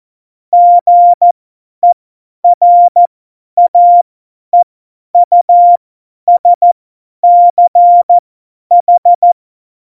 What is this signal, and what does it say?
Morse code, heard as a keyed tone, about 14 wpm, spaced as for 10 wpm, 700 Hz. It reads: GERAEUSCH
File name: Morsekurs / Witze / Witz 09. Januar 2018, 50 F70